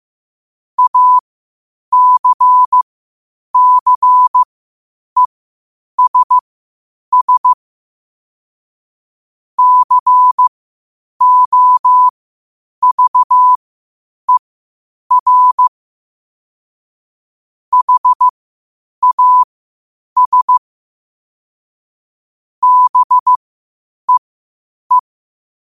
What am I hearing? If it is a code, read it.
Morse code, heard as a keyed tone, about 15 wpm, spaced as for 8 wpm, 1000 Hz. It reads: ACCESS COVER HAS BEE
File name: qotdfiles / 8wpm